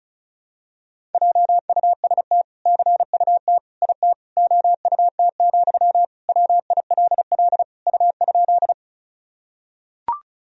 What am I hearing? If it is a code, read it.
Morse code, heard as a keyed tone, about 36 wpm, 700 Hz. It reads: JUST CUT IT OUT, WILL U?
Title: Sentences from Top 300 Words and Abbreviations 35wpm